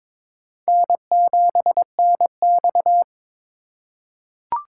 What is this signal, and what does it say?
Morse code, heard as a keyed tone, about 21 wpm, 700 Hz. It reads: N7NX
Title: USA Call Signs - 4 Characters - 4x Character Spacing - 22wpm